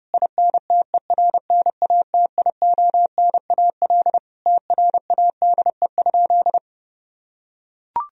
Morse code, heard as a keyed tone, about 30 wpm, 700 Hz. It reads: INTERNATIONAL TRADE?
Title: Sentences from Top 1000 Words - Review Part 4 - 30wpm